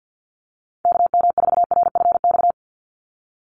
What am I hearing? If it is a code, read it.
Morse code, heard as a keyed tone, about 48 wpm, 700 Hz. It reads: KM4RFX
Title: USA Call Signs 50wpm